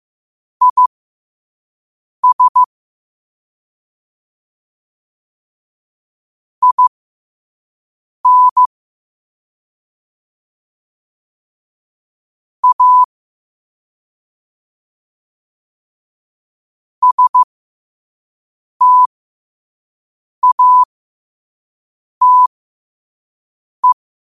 Morse code, heard as a keyed tone, about 15 wpm, spaced as for 5 wpm, 1000 Hz. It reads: IS IN A STATE
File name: qotdfiles / 5wpm